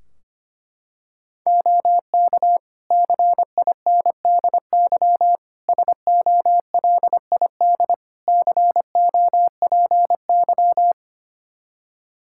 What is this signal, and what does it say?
Morse code, heard as a keyed tone, about 25 wpm, 700 Hz. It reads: OK CINDY SOLID COPY
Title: K1USN SST QSO Practice 25wpm